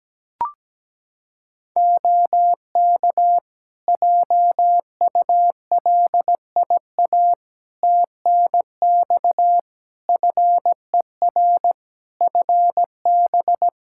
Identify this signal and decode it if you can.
Morse code, heard as a keyed tone, about 17 wpm, 700 Hz. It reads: OK JULIA TNX FER FB
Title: Practice QSO - Exchange 4 - Ending - 17wpm